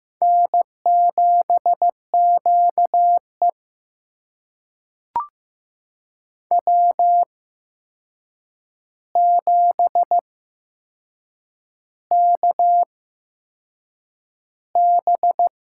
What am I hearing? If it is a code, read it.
Morse code, heard as a keyed tone, about 15 wpm, spaced as for 6 wpm, 700 Hz. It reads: N7QE W7KB
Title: USA Call Signs - 4 Characters - 8x Character Spacing - 15wpm